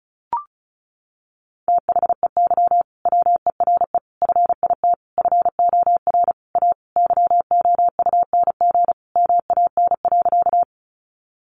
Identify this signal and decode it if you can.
Morse code, heard as a keyed tone, about 34 wpm, 700 Hz. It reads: THEY WERE FIT FOR A YOUNG MAN.
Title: Sentences from Top 800 Words 35wpm